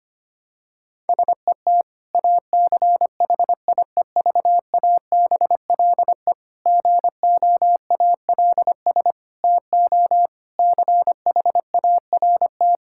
Morse code, heard as a keyed tone, about 25 wpm, 700 Hz. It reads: SET ACHIEVABLE GOALS TO CHART
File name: practice / 25wpm / Difficult Sentences with Hints - 25wpm